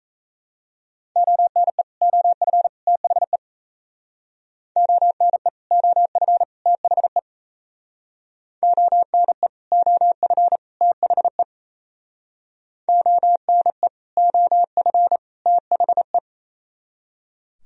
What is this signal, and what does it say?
Morse code, heard as a keyed tone, about 35 wpm, 700 Hz. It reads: ONE OF THE ONE OF THE ONE OF THE ONE OF THE